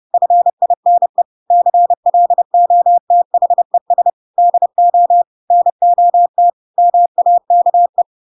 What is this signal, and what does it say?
Morse code, heard as a keyed tone, about 30 wpm, 700 Hz. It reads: FINE CLOTHES DO NOT MAKE
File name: Pinocchio / Pinocchio ch8 30 30 ik1hge